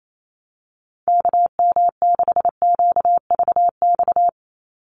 Morse code, heard as a keyed tone, about 28 wpm, 700 Hz. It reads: KM6QVX
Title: USA Call Signs 28wpm